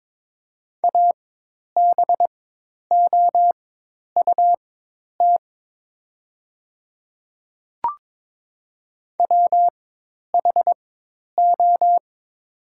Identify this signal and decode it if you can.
Morse code, heard as a keyed tone, about 22 wpm, spaced as for 10 wpm, 700 Hz. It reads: ABOUT WHO
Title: 4x Character Spacing - Top 100 Words - 22wpm